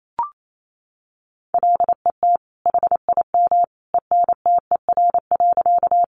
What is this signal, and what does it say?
Morse code, heard as a keyed tone, about 28 wpm, 700 Hz. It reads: LET HIM ENTER.
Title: Sentences from Top 1300 Words 28wpm